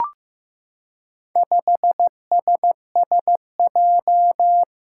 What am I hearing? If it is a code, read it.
Morse code, heard as a keyed tone, about 15 wpm, 700 Hz. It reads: 5SSJ
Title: Four Challenging Letter-Number Combinations 15wpm